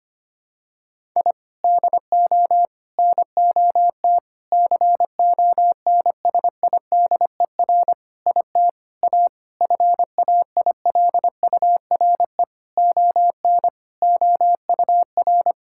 Morse code, heard as a keyed tone, about 25 wpm, 700 Hz. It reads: I DO NOT CONSIDER IT A FAILURE ON OUR
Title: Sentences from Top 1000 Words - Review Part 6 - 25wpm